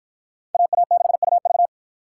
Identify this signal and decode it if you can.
Morse code, heard as a keyed tone, about 49 wpm, 700 Hz. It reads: AA6UV